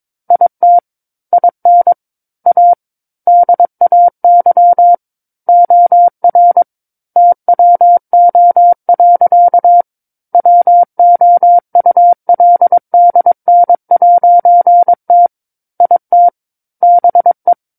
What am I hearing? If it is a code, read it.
Morse code, heard as a keyed tone, about 22 wpm, 700 Hz. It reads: IT IN A DAY OR TWO. WOULDN'T IT BE